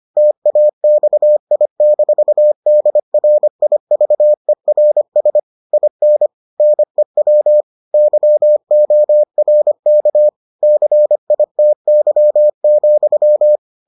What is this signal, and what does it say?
Morse code, heard as a keyed tone, about 25 wpm, 600 Hz. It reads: TAXI-DRIVERS IN NEW YORK CITY,